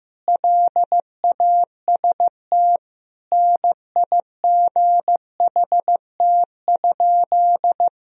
Morse code, heard as a keyed tone, about 15 wpm, 700 Hz. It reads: LAST NIGHT?
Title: Sentences from Top 1000 Words - Review Part 4 - 15wpm